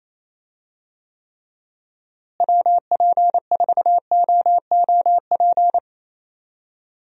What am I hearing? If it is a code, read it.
Morse code, heard as a keyed tone, about 27 wpm, 700 Hz. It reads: WP4OOP
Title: USA Call Signs - 6 Characters - Repeated 3x - 28wpm